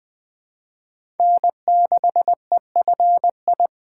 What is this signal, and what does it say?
Morse code, heard as a keyed tone, about 20 wpm, 700 Hz. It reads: N6EFI
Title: USA Call Signs - 5 Characters - 2x Character Spacing - 20wpm